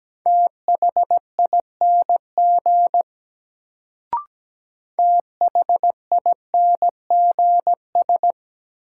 Morse code, heard as a keyed tone, about 17 wpm, 700 Hz. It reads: THING THINGS
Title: Sets of 3 Words - New Word Drill - 17wpm